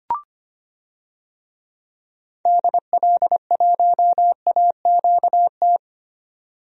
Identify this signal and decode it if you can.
Morse code, heard as a keyed tone, about 25 wpm, 700 Hz. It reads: DL1AQT
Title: DXCC-100 Call Signs - 6 Characters - 25wpm